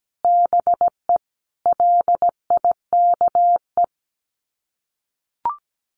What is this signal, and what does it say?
Morse code, heard as a keyed tone, about 17 wpm, 700 Hz. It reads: BE LIKE
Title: Sets of 5 Words Encore 17wpm